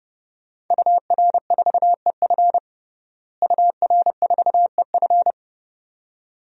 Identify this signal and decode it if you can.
Morse code, heard as a keyed tone, about 31 wpm, 700 Hz. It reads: UR4EF UR4EF
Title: DXCC-100 Call Signs - 5 Characters - Repeated 2x - 30wpm